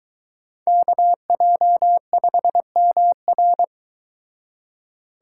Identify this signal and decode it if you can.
Morse code, heard as a keyed tone, about 22 wpm, 700 Hz. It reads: KJ5MR